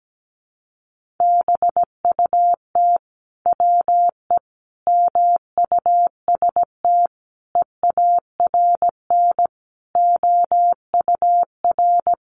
Morse code, heard as a keyed tone, about 17 wpm, 700 Hz. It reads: BUT WE MUST EARN OUR
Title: Sentences from Top 1500 Words 17wpm